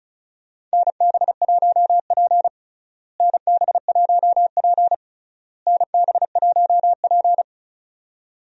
Morse code, heard as a keyed tone, about 36 wpm, 700 Hz. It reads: NB1P NB1P NB1P
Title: USA Call Signs - 4 Characters - Repeated 3x - 35wpm